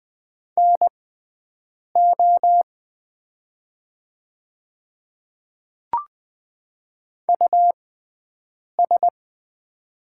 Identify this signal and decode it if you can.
Morse code, heard as a keyed tone, about 20 wpm, spaced as for 7 wpm, 700 Hz. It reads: NO US
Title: 6x Character Spacing - Top 100 Words - 20wpm